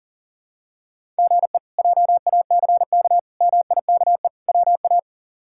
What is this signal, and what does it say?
Morse code, heard as a keyed tone, about 40 wpm, 700 Hz. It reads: GE JACK MIKE WA
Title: K1USN SST QSO Practice - 40wpm